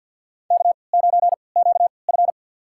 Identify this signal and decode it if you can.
Morse code, heard as a keyed tone, about 52 wpm, spaced as for 28 wpm, 700 Hz. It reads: K9QF
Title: USA Call Signs - 4 Characters - 3x Character Spacing - 50wpm